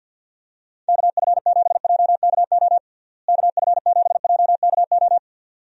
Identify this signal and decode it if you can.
Morse code, heard as a keyed tone, about 50 wpm, 700 Hz. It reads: KF7JKO KF7JKO
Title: USA Call Signs - 6 Characters - Repeated 2x - 50wpm